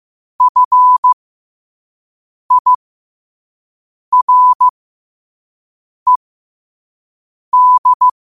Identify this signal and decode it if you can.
Morse code, heard as a keyed tone, about 15 wpm, spaced as for 5 wpm, 1000 Hz. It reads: FIRED
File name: qotdfiles / 5wpm